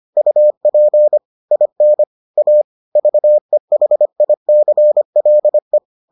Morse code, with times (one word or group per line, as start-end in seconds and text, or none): 0.2-1.2 UP
1.5-2.0 IN
2.4-2.6 A
3.0-5.8 VEHICLE